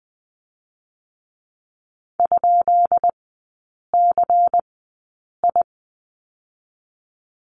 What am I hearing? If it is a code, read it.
Morse code, heard as a keyed tone, about 20 wpm, spaced as for 8 wpm, 700 Hz. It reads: ?CI
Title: 152 Lesson 25e - Letter p - Three Characters 20wpm